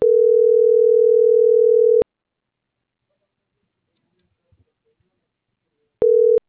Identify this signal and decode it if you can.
Ringback tone (North America), 440 + 480 Hz.